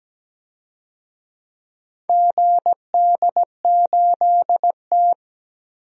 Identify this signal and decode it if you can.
Morse code, heard as a keyed tone, about 17 wpm, 700 Hz. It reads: GD8T
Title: DXCC-100 Call Signs - 4 Characters - 17wpm